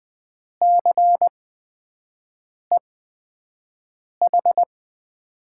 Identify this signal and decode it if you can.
Morse code, heard as a keyed tone, about 20 wpm, spaced as for 5 wpm, 700 Hz. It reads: CEH
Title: USA Call Signs - 8x Character Spacing - 20wpm